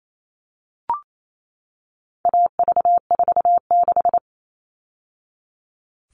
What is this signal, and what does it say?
Morse code, heard as a keyed tone, about 28 wpm, 700 Hz. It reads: AV46
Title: All Cut Numbers - 4-Digit 28wpm